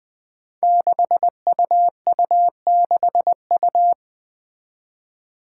Morse code, heard as a keyed tone, about 20 wpm, 700 Hz. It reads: 6UU6U
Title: Five Challenging Letter-Number Combinations 20wpm